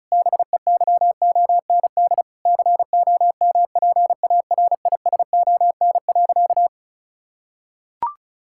Morse code, heard as a keyed tone, about 36 wpm, 700 Hz. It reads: BEYOND COMPARISON.
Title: Sentences from Top 1600 Words 35wpm